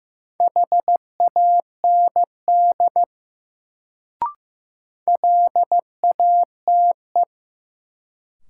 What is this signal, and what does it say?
Morse code, heard as a keyed tone, about 15 wpm, 700 Hz. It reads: HAND LATE
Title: Sets of 3 Words - New Word Drill - 15wpm